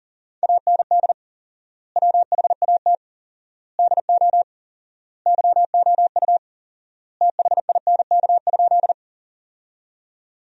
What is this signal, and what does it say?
Morse code, heard as a keyed tone, about 40 wpm, 700 Hz. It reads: AND WHAT DO YOU THINK?
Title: 4x Word-Spacing - Sentences from Top 200 Words - 40wpm